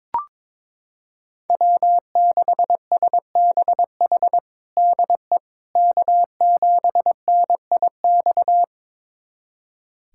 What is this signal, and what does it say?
Morse code, heard as a keyed tone, about 22 wpm, 700 Hz. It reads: W6SBH DE K7NIX
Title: USA Callsign DE Callsign 22wpm